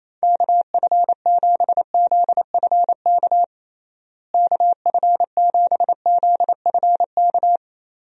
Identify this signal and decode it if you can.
Morse code, heard as a keyed tone, about 28 wpm, 700 Hz. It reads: KF7ZFK KF7ZFK
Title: USA Call Signs - 6 Characters - Repeated 3x - 28wpm